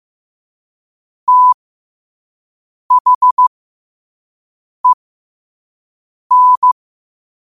Morse code, heard as a keyed tone, about 15 wpm, spaced as for 5 wpm, 1000 Hz. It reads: THEN
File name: qotdfiles / 5wpm